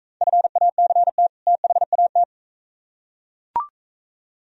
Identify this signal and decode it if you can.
Morse code, heard as a keyed tone, about 41 wpm, 700 Hz. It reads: FACT THAT